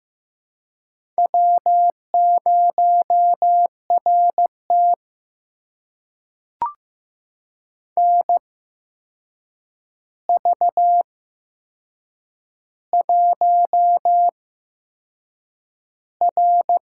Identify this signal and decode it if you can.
Morse code, heard as a keyed tone, about 15 wpm, spaced as for 6 wpm, 700 Hz. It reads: W0RT NV1R
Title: USA Call Signs - 4 Characters - 8x Character Spacing - 15wpm